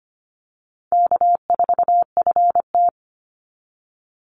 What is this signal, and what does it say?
Morse code, heard as a keyed tone, about 25 wpm, 700 Hz. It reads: K4FT
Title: Call Signs - 25wpm